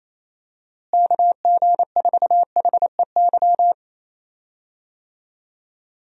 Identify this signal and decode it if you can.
Morse code, heard as a keyed tone, about 28 wpm, 700 Hz. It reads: KG4HEY
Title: USA Call Signs - 6 Characters - Repeated 3x - 28wpm